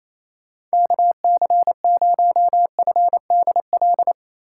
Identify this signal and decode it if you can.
Morse code, heard as a keyed tone, about 28 wpm, 700 Hz. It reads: KC0FDL
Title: USA Call Signs - 6 Characters - 6x Character Spacing - 28wpm